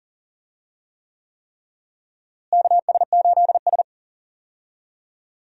Morse code, heard as a keyed tone, about 37 wpm, 700 Hz. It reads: KS8S
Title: USA Call Signs - 4 Characters - Repeated 3x - 40wpm